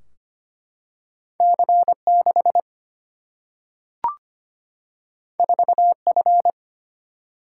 Morse code, heard as a keyed tone, about 25 wpm, 700 Hz. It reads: C 6 4 F
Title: Single-Letter Single-Number 25wpm